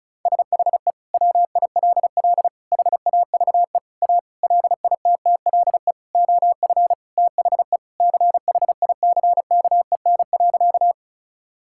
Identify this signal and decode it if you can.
Morse code, heard as a keyed tone, about 36 wpm, 700 Hz. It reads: SHE WILL HAVE A LITTLE OF THE CHICKEN.
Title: Sentences from Top 1000 Words - Review Part 5 - 35wpm